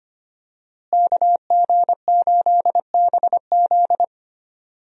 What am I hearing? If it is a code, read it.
Morse code, heard as a keyed tone, about 25 wpm, 700 Hz. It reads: KG8BZ
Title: USA Call Signs - 5 Characters - Repeated 2x - 25wpm